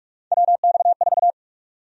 Morse code, heard as a keyed tone, about 47 wpm, 700 Hz. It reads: WX4
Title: USA Call Sign Prefixes - 45wpm